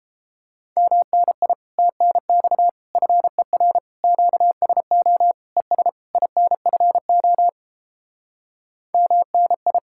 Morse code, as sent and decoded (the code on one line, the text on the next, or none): -- -. .. / - -. -..- / ..-. .-. / --.- ... --- / . ... / .. -. ..-. --- / -- -. ..
MNI TNX FER QSO ES INFO MNI